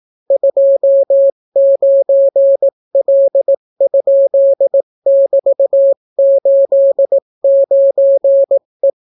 Morse code, heard as a keyed tone, about 18 wpm, 550 Hz. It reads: 29L?<BT>89E